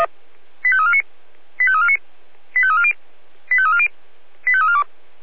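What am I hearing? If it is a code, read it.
Five-tone selective call (ZVEI), heard as quick runs of five tones, tones of about 80 ms. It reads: 85349 85349 85340 85340 85342
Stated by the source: ZVEI five-tone selective call received on SDR.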